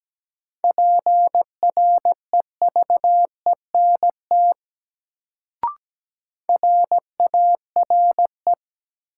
Morse code, heard as a keyed tone, about 17 wpm, 700 Hz. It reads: PREVENT RARE
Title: Sentences from Top 1100 Words - New Word Drill Alt - 17wpm